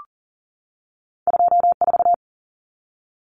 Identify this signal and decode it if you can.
Morse code, heard as a keyed tone, about 40 wpm, 700 Hz. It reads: 24
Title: Common Cut Numbers - 2-Digit 40wpm